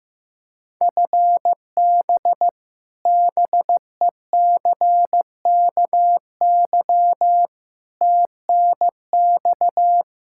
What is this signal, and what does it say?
Morse code, heard as a keyed tone, about 16 wpm, 700 Hz. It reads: FB BECKY TNX
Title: Practice QSO - Exchange 3 - More Info - 15wpm